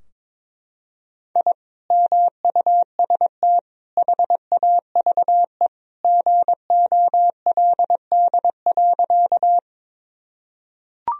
Morse code, heard as a keyed tone, about 22 wpm, 700 Hz. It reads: I MUST HAVE GOLD.
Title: Sentences from Top 1000 Words - Review Part 3 - 22wpm